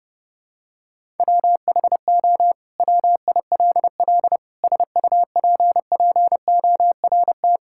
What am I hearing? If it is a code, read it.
Morse code, heard as a keyed tone, about 32 wpm, 700 Hz. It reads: WHO WILL SUPPORT
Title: Sentences from Top 1000 Words - Review Part 5 - 30wpm